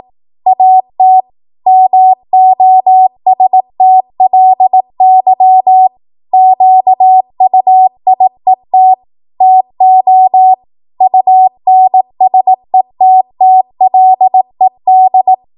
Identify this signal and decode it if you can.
Morse code, heard as a keyed tone, about 18 wpm, 750 Hz. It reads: AT MOSTLY QUIET TO UNSETTLED
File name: Morse / 18WPM